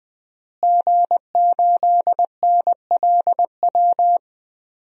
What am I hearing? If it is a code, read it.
Morse code, heard as a keyed tone, about 20 wpm, 700 Hz. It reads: G8NLW
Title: DXCC-100 Call Signs - 5 Characters - Repeated 2x - 20wpm